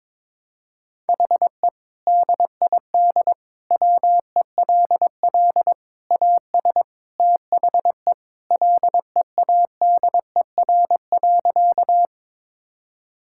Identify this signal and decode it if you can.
Morse code, heard as a keyed tone, about 22 wpm, 700 Hz. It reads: HE DID WELL AS THE LEADER.
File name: practice / 22wpm / Sentences with 3-Word Phrases - 22wpm